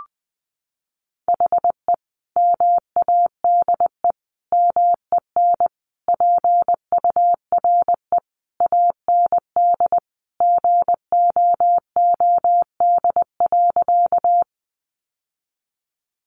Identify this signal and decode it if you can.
Morse code, heard as a keyed tone, about 20 wpm, 700 Hz. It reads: HE MADE MEN PURE AND GOOD.
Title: Sentences from Top 1500 Words 20wpm